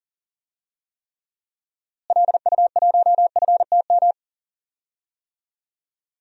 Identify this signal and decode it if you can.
Morse code, heard as a keyed tone, about 40 wpm, 700 Hz. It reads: LU1FTM
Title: DXCC-100 Call Signs - 6 Characters - Repeated 3x - 40wpm